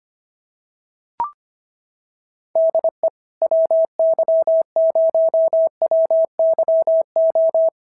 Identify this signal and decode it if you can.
Morse code, heard as a keyed tone, about 25 wpm, 650 Hz. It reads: DE WY0WYO
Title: SOTA QSO Practice 25wpm